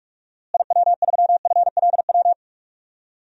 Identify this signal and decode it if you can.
Morse code, heard as a keyed tone, about 42 wpm, 700 Hz. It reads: IW3FLW
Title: DXCC-100 Call Signs - 6 Characters - Repeated 3x - 45wpm